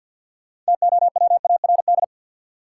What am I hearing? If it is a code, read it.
Morse code, heard as a keyed tone, about 51 wpm, 700 Hz. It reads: TOWARD
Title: Sentences from Top 1000 Words - Review Part 6 - New Word Drill - 50wpm